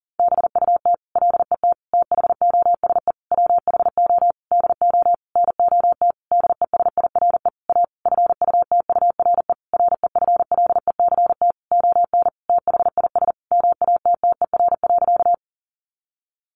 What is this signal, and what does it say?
Morse code, heard as a keyed tone, about 37 wpm, 700 Hz. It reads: BUT LET THOSE WHO DO NOT DESIRE A FUTURE REFLECT ON THIS MATTER.
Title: Sentences from Top 1500 Words 40wpm